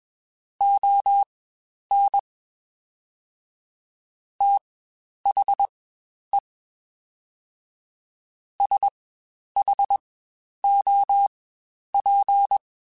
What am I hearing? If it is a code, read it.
Morse code, heard as a keyed tone, about 21 wpm, spaced as for 10 wpm, 800 Hz. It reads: ON THE SHOP
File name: Spaced Characters / Spaced Characters-mp3s / Weather 4 24 sp @20